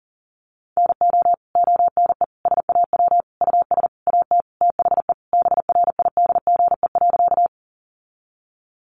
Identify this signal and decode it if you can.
Morse code, heard as a keyed tone, about 40 wpm, 700 Hz. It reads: NO ONE SAW US AT THE BRIDGE.
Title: Sentences from Top 1200 Words 40wpm